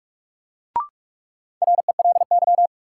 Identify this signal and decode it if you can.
Morse code, heard as a keyed tone, about 46 wpm, 700 Hz. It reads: RELY
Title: Sentences from Top 1200 Words - New Word Drill - 45wpm